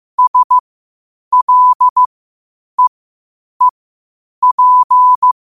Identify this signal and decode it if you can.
Morse code, heard as a keyed tone, about 15 wpm, spaced as for 8 wpm, 1000 Hz. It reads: SLEEP
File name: qotdfiles / 8wpm